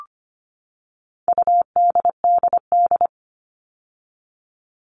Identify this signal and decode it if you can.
Morse code, heard as a keyed tone, about 25 wpm, 700 Hz. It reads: UDDD
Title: All Cut Numbers - 4-Digit 25wpm